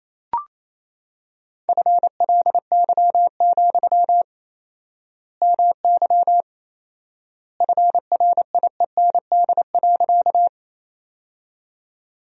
Morse code, heard as a keyed tone, about 28 wpm, 700 Hz. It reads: FLY, MY FRIEND.
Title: 4x Word-Spacing - Sentences from Top 500 Words - 28wpm